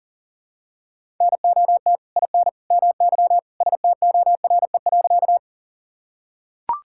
Morse code, heard as a keyed tone, about 40 wpm, 700 Hz. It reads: NOT IN MY STORE.
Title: Sentences from Top 1000 Words - Review Part 5 - 40wpm